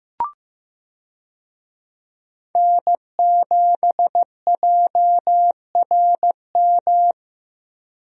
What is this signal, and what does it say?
Morse code, heard as a keyed tone, about 15 wpm, 700 Hz. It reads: N7JRM
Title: USA Call Signs - 5 Characters - Repeated 2x - 15wpm